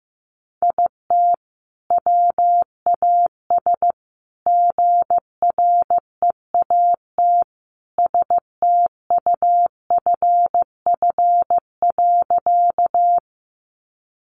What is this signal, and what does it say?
Morse code, heard as a keyed tone, about 15 wpm, 700 Hz. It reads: IT WAS GREAT STUFF.
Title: Sentences from Top 1400 Words 15wpm